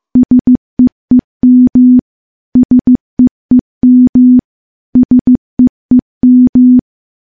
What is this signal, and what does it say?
Morse code, heard as a keyed tone, about 16 wpm, 250 Hz. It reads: SEEM SEEM SEEM
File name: morse-training-15WPM